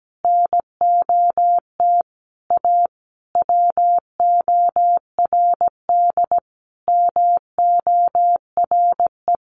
Morse code, heard as a keyed tone, about 17 wpm, 700 Hz. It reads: NOT A WORD MORE
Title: Sentences from Top 800 Words 17wpm